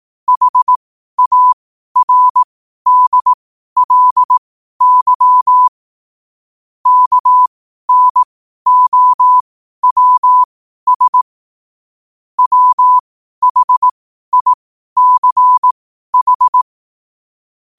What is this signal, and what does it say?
Morse code, heard as a keyed tone, about 18 wpm, spaced as for 12 wpm, 1000 Hz. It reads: HARDLY KNOWS WHICH
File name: qotdfiles / 12wpm